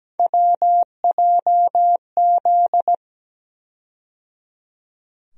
Morse code, heard as a keyed tone, about 17 wpm, 700 Hz. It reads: WJZ
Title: USA Call Sign Postfixes - 3 Letters 17wpm